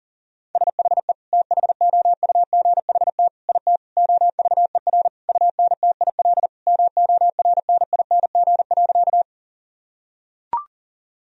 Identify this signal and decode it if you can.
Morse code, heard as a keyed tone, about 37 wpm, 700 Hz. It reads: SHE THOUGHT IT OVER UNTIL MORNING.
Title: Sentences from Top 1000 Words - Review Part 1 - 40wpm